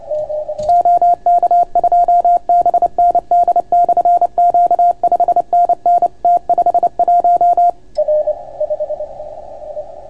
Morse code, heard as a keyed tone, about 28 wpm, 690 Hz. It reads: OK2BND/Q 5NN T51